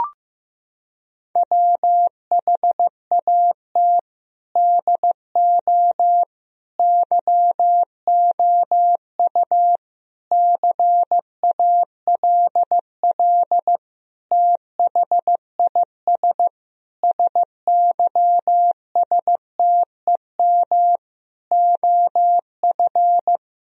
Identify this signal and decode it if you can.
Morse code, heard as a keyed tone, about 15 wpm, 700 Hz. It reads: WHAT DO YOU CALL THIS SYSTEM OF
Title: Sentences from Top 1000 Words - Review Part 3 - 15wpm